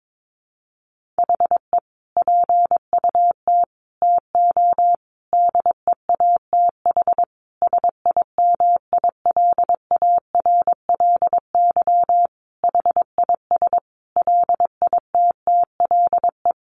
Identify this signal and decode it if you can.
Morse code, heard as a keyed tone, about 22 wpm, 700 Hz. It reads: HE PUT TO DEATH SIMILARLY HIS LITTLE
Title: Sentences from Top 1500 Words 22wpm